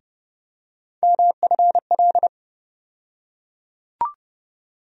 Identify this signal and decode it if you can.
Morse code, heard as a keyed tone, about 30 wpm, 700 Hz. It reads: MFL
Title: USA Call Sign Postfixes - 3 Letters 30wpm